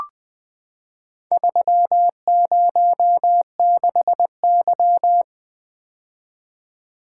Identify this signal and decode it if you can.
Morse code, heard as a keyed tone, about 20 wpm, 700 Hz. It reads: 306Y
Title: Four Letter-Number Combinations - 20wpm